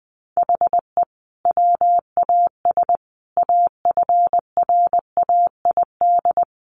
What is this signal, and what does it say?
Morse code, heard as a keyed tone, about 19 wpm, 700 Hz. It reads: HE WAS AFRAID
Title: Sentences from Top 1500 Words 20wpm